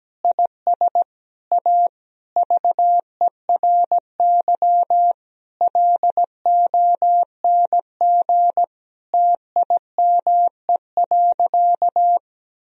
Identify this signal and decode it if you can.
Morse code, heard as a keyed tone, about 17 wpm, 700 Hz. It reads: IS A VERY LONG TIME.
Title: Sentences from Top 1000 Words - Review Part 3 - 17wpm